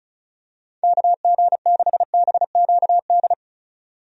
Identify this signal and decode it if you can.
Morse code, heard as a keyed tone, about 34 wpm, 700 Hz. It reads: KG6BQD
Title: USA Call Signs - 6 Characters - 35wpm